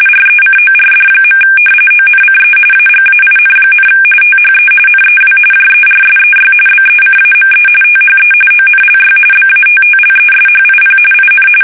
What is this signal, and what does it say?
RTTY, about 45.45 baud. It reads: RECORD. ONCE AGAIN, THE NEW JERSEY BECAME THE ONLY BATTLESHIP IN
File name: fsk850 / battleshipnj